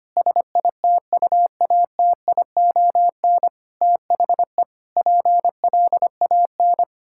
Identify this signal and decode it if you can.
Morse code, heard as a keyed tone, about 25 wpm, 700 Hz. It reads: SITUATION THE PLAN
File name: practice / 25wpm / Sentences with 9-Word Phrases - 25wpm